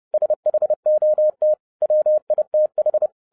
Morse code, heard as a keyed tone, about 30 wpm, 610 Hz. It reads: SHOT WITH